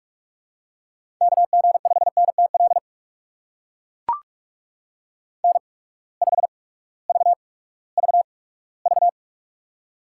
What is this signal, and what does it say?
Morse code, heard as a keyed tone, about 44 wpm, spaced as for 20 wpm, 700 Hz. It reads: KG5NTL N5VVV
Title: USA Call Signs - 8x Character Spacing - 45wpm